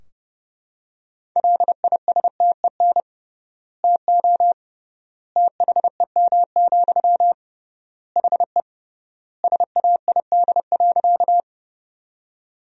Morse code, heard as a keyed tone, about 32 wpm, 700 Hz. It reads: LISTEN TO THEM, HE SAID.
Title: 3x Word-Spacing - Sentences from Top 400 Words - 30wpm